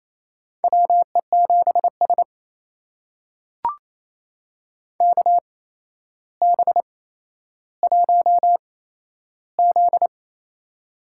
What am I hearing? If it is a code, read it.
Morse code, heard as a keyed tone, about 28 wpm, spaced as for 12 wpm, 700 Hz. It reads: WE7S KB1Z
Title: USA Call Signs - 4 Characters - 8x Character Spacing - 28wpm